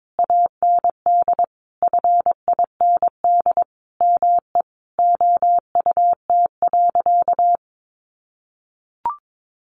Morse code, heard as a keyed tone, about 22 wpm, 700 Hz. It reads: AND FIND ME OUT.
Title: Sentences from Top 500 Words 22wpm